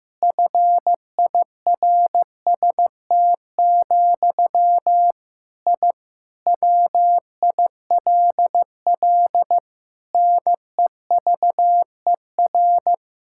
Morse code, heard as a keyed tone, about 15 wpm, 700 Hz. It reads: FIRST, I WILL NEVER
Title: Sentences from Top 1000 Words - Review Part 6 - 15wpm